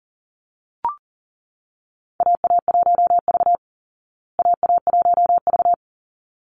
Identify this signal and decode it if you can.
Morse code, heard as a keyed tone, about 40 wpm, 700 Hz. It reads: AA1V AA1V
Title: Call Signs Repeated 3x - 40wpm